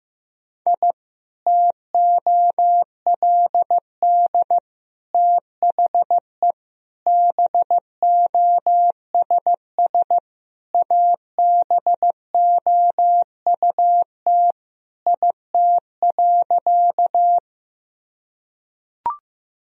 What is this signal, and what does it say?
Morse code, heard as a keyed tone, about 16 wpm, 700 Hz. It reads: I TOLD THE BOSS ABOUT IT.
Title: Sentences from Top 1000 Words - Review Part 4 - 15wpm